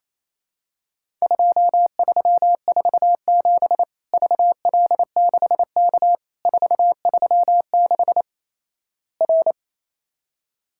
Morse code, heard as a keyed tone, about 28 wpm, 700 Hz. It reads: 2347 VL6K 436 R
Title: Worked All Europe DX Contest - QTC Exchange - 28wpm